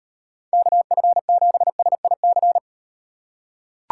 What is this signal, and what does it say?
Morse code, heard as a keyed tone, about 38 wpm, 700 Hz. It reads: KF7SIC